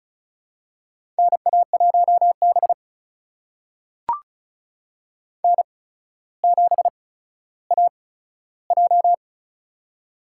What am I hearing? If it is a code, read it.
Morse code, heard as a keyed tone, about 36 wpm, spaced as for 15 wpm, 700 Hz. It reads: NA1B N7AJ
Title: USA Call Signs - 4 Characters - 8x Character Spacing - 35wpm